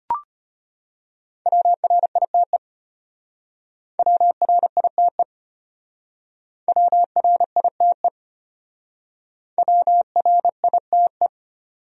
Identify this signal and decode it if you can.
Morse code, heard as a keyed tone, about 31 wpm, 700 Hz. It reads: WRITE WRITE WRITE WRITE